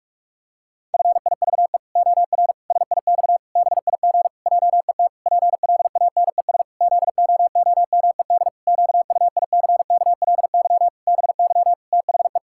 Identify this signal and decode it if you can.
Morse code, heard as a keyed tone, about 46 wpm, 700 Hz. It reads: FIVE OR SIX BIG JET PLANES ZOOMED QUICKLY BY THE
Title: Pangram Challenge v2 - 45wpm